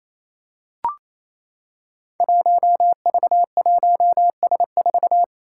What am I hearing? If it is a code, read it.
Morse code, heard as a keyed tone, about 28 wpm, 700 Hz. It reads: 1V1S4
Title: Five Challenging Letter-Number Combinations 28wpm